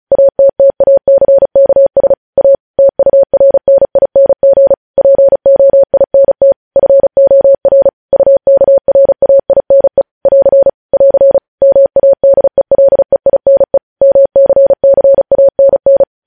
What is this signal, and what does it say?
Morse code, heard as a keyed tone, about 36 wpm, 550 Hz. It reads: ATTACKS A TURNING POINT FOR UKRAINE <AR> <AR> MADELEINE MCCANN
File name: podcast / News headlines-1x-35x35